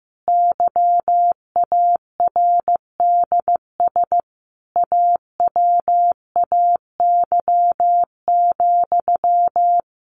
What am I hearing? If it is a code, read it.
Morse code, heard as a keyed tone, about 15 wpm, 700 Hz. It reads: YARDS AWAY,
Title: A Sound Of Thunder 15wpm